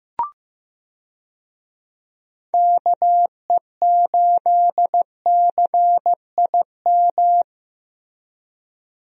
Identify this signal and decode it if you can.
Morse code, heard as a keyed tone, about 15 wpm, 700 Hz. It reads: KE8CIM
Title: USA Call Signs - 6 Characters - Repeated 3x - 15wpm